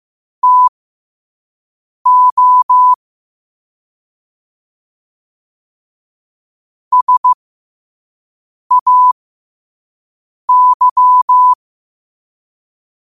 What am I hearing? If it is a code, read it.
Morse code, heard as a keyed tone, about 15 wpm, spaced as for 5 wpm, 1000 Hz. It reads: TO SAY
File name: qotdfiles / 5wpm